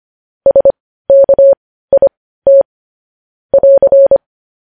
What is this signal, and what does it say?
Morse code, heard as a keyed tone, about 25 wpm, spaced as for 15 wpm, 550 Hz. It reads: SKIT <AR>
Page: news headlines in morse code at 15 wpm